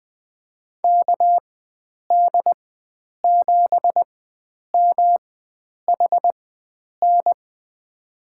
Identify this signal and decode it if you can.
Morse code, heard as a keyed tone, about 20 wpm, spaced as for 9 wpm, 700 Hz. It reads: KD7MHN
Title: USA Call Signs - 6 Characters - 4x Character Spacing - 20wpm